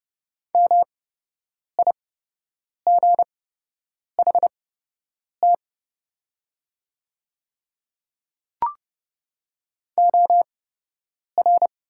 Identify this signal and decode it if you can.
Morse code, heard as a keyed tone, about 30 wpm, spaced as for 8 wpm, 700 Hz. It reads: MIGHT OR
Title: 8x Character Spacing - Top 300 Words - 30wpm